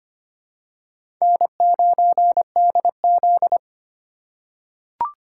Morse code, heard as a keyed tone, about 25 wpm, 700 Hz. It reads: N9DZ
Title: USA Call Signs - 4 Characters - 4x Character Spacing - 25wpm